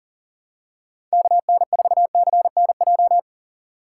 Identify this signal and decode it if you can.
Morse code, heard as a keyed tone, about 40 wpm, 700 Hz. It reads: KN4CNJ
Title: USA Call Signs - 6 Characters - 3x Character Spacing - 40wpm